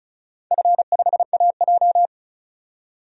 Morse code, heard as a keyed tone, about 36 wpm, 700 Hz. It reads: F5AJ
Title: DXCC-100 Call Signs - 4 Characters - 35wpm